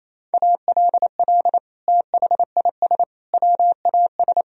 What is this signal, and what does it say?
Morse code, heard as a keyed tone, about 28 wpm, 700 Hz. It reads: ALL THIS WAS
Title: Sentences from Top 1000 Words - Review Part 2 - 28wpm